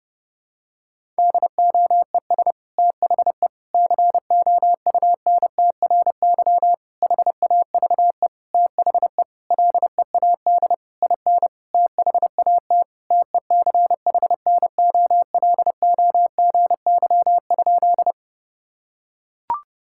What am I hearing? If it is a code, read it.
Morse code, heard as a keyed tone, about 30 wpm, 700 Hz. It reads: DOES THE COUNTRY HAVE THE LEAD IN THAT TECHNOLOGY?
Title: Sentences from Top 1000 Words - Review Part 5 - 30wpm